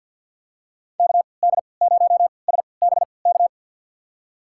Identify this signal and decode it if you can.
Morse code, heard as a keyed tone, about 46 wpm, spaced as for 27 wpm, 700 Hz. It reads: KD0SBK
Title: USA Call Signs - 6 Characters - 3x Character Spacing - 50wpm